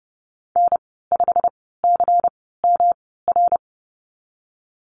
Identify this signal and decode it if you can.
Morse code, heard as a keyed tone, about 32 wpm, spaced as for 17 wpm, 700 Hz. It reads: N5CMR
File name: Lesson 19g - Letter w - Call Signs 30wpm